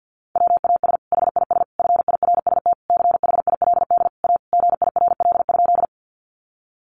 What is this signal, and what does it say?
Morse code, heard as a keyed tone, about 50 wpm, 700 Hz. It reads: WAS HIS FIRST CHILD A GIRL?
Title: Sentences from Top 800 Words 50wpm